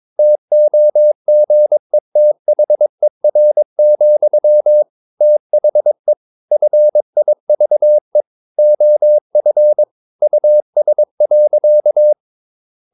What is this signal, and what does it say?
Morse code, heard as a keyed tone, about 22 wpm, 600 Hz. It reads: TOGETHER, THE FIVE OF US.
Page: Romanen Never af Ken Follet på engelsk.